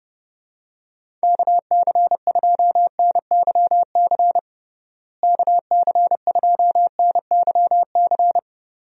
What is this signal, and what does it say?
Morse code, heard as a keyed tone, about 32 wpm, 700 Hz. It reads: KC2NYC KC2NYC
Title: USA Call Signs - 6 Characters - Repeated 3x - 30wpm